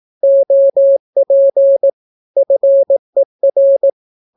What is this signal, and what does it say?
Morse code, heard as a keyed tone, about 18 wpm, 550 Hz. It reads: OP FER
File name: naukapodstaw / skroty / skroty